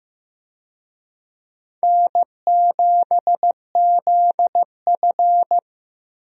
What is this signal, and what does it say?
Morse code, heard as a keyed tone, about 15 wpm, 700 Hz. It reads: N7ZF
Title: USA Call Signs - 4 Characters - Repeated 2x - 15wpm